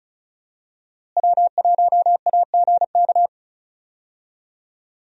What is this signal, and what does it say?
Morse code, heard as a keyed tone, about 36 wpm, 700 Hz. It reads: W1AGK
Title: USA Call Signs - 5 Characters - Repeated 2x - 35wpm